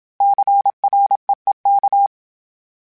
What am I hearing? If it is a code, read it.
Morse code, heard as a keyed tone, about 27 wpm, 800 Hz. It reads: CREEK
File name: fromSMs / FromSMs mp3s / Cities@25